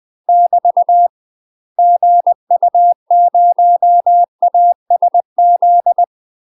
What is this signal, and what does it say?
Morse code, heard as a keyed tone, about 20 wpm, 700 Hz. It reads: <BT> GU0ASZ
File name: UK MASTER QSOs TRAINING / 20WPM / UK MASTER QSOs 0007